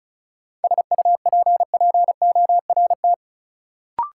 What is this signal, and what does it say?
Morse code, heard as a keyed tone, about 36 wpm, 700 Hz. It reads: SUPPORT
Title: Sentences from Top 800 Words - New Word Drill - 35wpm